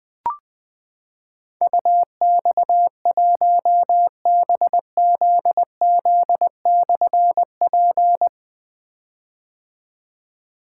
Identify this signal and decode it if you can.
Morse code, heard as a keyed tone, about 20 wpm, 700 Hz. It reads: UX1BZZ/P
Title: Difficult Call Signs 20wpm